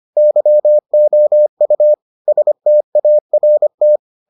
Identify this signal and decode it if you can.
Morse code, heard as a keyed tone, about 25 wpm, 600 Hz. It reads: YOU START